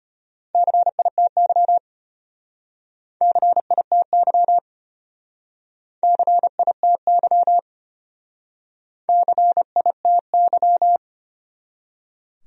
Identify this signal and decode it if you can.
Morse code, heard as a keyed tone, about 33 wpm, 700 Hz. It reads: CITY CITY CITY CITY